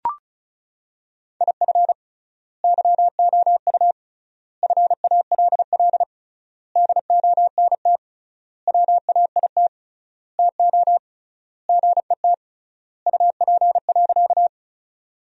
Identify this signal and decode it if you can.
Morse code, heard as a keyed tone, about 34 wpm, 700 Hz. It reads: IF YOU FALL DONT WAIT TO GET UP.
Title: 3x Word-Spacing - Sentences from Top 500 Words - 35wpm